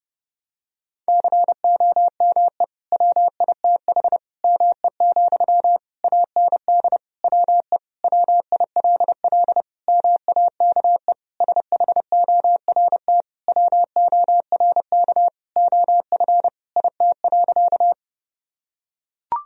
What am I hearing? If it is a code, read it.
Morse code, heard as a keyed tone, about 30 wpm, 700 Hz. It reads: COME WITH ME, AND WE WILL MAKE SHORT WORK OF IT.
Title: Sentences from Top 1000 Words - Review Part 2 - 30wpm